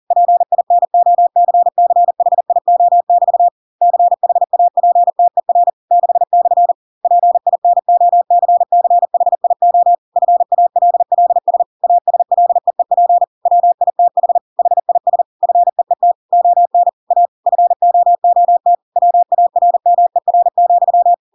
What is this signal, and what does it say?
Morse code, heard as a keyed tone, about 40 wpm, 700 Hz. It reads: PINOCCHIO<BT> CHAPTER 6/ PINOCCHIO FALLS ASLEEP WITH HIS FEET ON A FOOT WARMER,